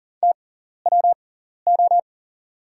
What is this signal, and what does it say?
Morse code, heard as a keyed tone, about 40 wpm, spaced as for 14 wpm, 690 Hz. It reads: TWO
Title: 6x Character Spacing - Top 300 Words - 40wpm